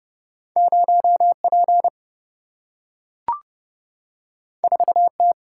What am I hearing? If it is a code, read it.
Morse code, heard as a keyed tone, about 31 wpm, 700 Hz. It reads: 0P 4T
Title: Single-Letter Single-Number 30wpm